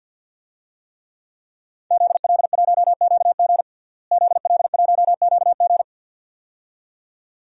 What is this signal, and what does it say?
Morse code, heard as a keyed tone, about 51 wpm, 700 Hz. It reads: ZL1QG ZL1QG
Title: DXCC-100 Call Signs - 5 Characters - Repeated 2x - 50wpm